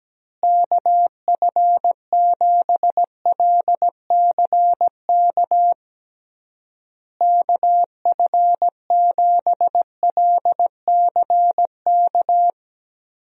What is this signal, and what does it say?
Morse code, heard as a keyed tone, about 17 wpm, 700 Hz. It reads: KF7LCK KF7LCK
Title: USA Call Signs - 6 Characters - Repeated 3x - 17wpm